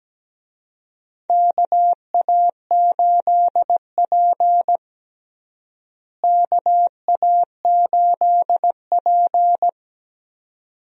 Morse code, heard as a keyed tone, about 17 wpm, 700 Hz. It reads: KA8P KA8P
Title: USA Call Signs - 4 Characters - Repeated 2x - 17wpm